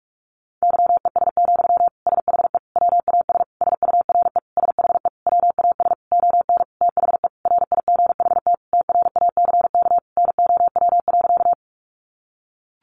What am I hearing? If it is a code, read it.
Morse code, heard as a keyed tone, about 46 wpm, 700 Hz. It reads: YES, SHE WAS SURE SHE WAS ON THE RIGHT TRACK NOW.
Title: Sentences from Top 1100 Words 45wpm